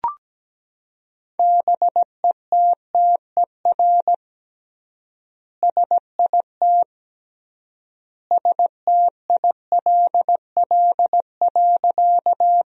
Morse code, heard as a keyed tone, about 17 wpm, 700 Hz. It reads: BETTER SIT STILL.
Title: 3x Word-Spacing - Sentences from Top 500 Words - 17wpm